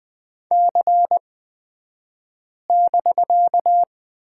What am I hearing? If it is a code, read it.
Morse code, heard as a keyed tone, about 20 wpm, spaced as for 5 wpm, 700 Hz. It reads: C<BK>
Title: 249 Lesson 40d - BK - Two Characters 20wpm